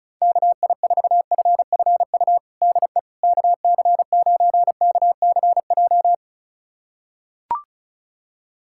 Morse code, heard as a keyed tone, about 34 wpm, 700 Hz. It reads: KI4FFU DE KC9KCJ
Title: USA Callsign DE Callsign 35wpm